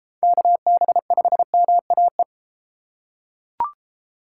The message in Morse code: KB5MAE

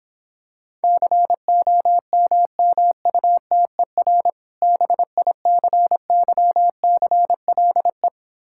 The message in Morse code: COMMUTER BICYCLE

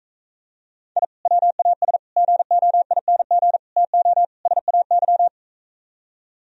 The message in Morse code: I WAS GOING TO SAY

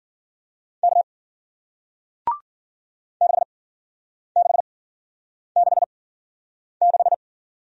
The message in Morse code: K 6666

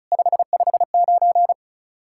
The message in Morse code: 559

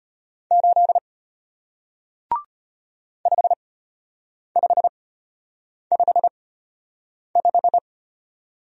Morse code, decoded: 8 5555

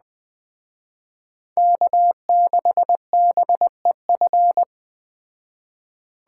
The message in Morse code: K6BEF

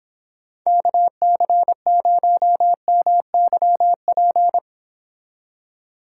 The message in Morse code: KC0MYP

